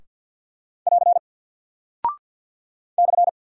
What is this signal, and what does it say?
Morse code, heard as a keyed tone, about 48 wpm, 700 Hz. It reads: <AR> /
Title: Single Character (Letter:Number:Punctuation:Prosign) Alt 50wpm